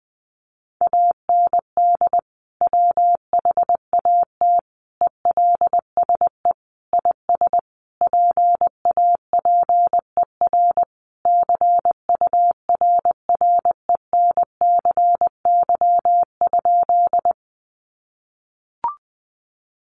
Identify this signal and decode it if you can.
Morse code, heard as a keyed tone, about 20 wpm, 700 Hz. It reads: AND WHAT ELSE IS PAPER CURRENCY?
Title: Sentences from Top 1500 Words 20wpm